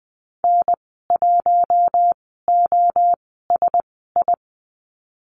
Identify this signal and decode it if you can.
Morse code, heard as a keyed tone, about 20 wpm, spaced as for 15 wpm, 700 Hz. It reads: N1OSI
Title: 048 Lesson 9g - Number 4 - Call Signs 20wpm